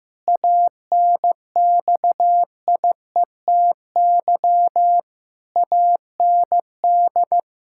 Morse code, heard as a keyed tone, about 16 wpm, 700 Hz. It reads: ANXIETY AND
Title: Pangram Challenge v2 - 15wpm